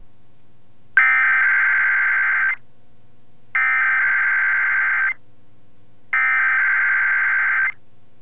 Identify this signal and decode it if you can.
Emergency Alert System (SAME) tones (header bursts), heard as harsh buzzing bursts. It is heard from 1.0 to 7.7 s.